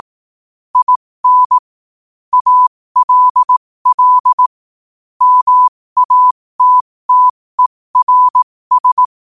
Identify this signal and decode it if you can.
Morse code, heard as a keyed tone, about 18 wpm, spaced as for 15 wpm, 1000 Hz. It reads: IN ALL MATTERS
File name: qotdfiles / 15wpm